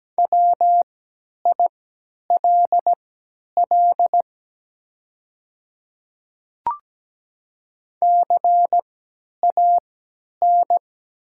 Morse code, heard as a keyed tone, about 17 wpm, spaced as for 10 wpm, 700 Hz. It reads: WILL CAN
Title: 3x Character Spacing - Top 300 Words - 17wpm